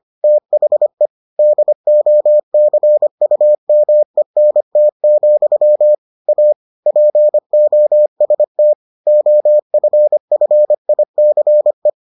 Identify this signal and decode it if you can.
Morse code, heard as a keyed tone, about 25 wpm, 600 Hz. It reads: THE DOCUMENT, A POST OFFICE